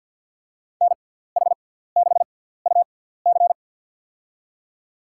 Morse code, heard as a keyed tone, about 52 wpm, spaced as for 17 wpm, 700 Hz. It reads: NH6UC
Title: USA Call Signs - 5 Characters - 6x Character Spacing - 50wpm